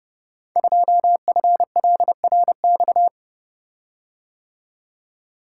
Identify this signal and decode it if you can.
Morse code, heard as a keyed tone, about 30 wpm, 700 Hz. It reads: 2FLRX
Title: Code Groups - 5 Letter:Numbers - 30wpm